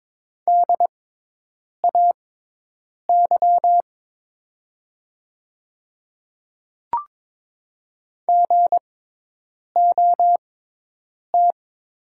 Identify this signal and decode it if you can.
Morse code, heard as a keyed tone, about 22 wpm, spaced as for 8 wpm, 700 Hz. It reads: DAY GOT
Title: 6x Character Spacing - Top 500 Words - 22wpm